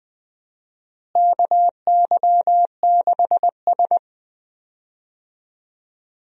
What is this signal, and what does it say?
Morse code, heard as a keyed tone, about 20 wpm, 700 Hz. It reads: KY6S
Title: USA Call Signs - 4 Characters - Repeated 2x - 20wpm